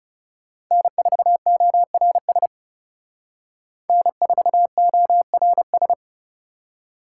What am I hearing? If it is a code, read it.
Morse code, heard as a keyed tone, about 38 wpm, 700 Hz. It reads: N4ORS N4ORS